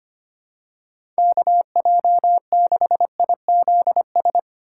Morse code, heard as a keyed tone, about 25 wpm, 700 Hz. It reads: KJ6IZS
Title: USA Call Signs - 6 Characters - 3x Character Spacing - 25wpm